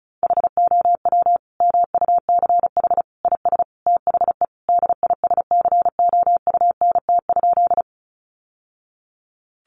Morse code, heard as a keyed tone, about 34 wpm, 700 Hz. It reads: HOW MUCH IS THE DISCOUNT?
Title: Sentences from Top 1200 Words 35wpm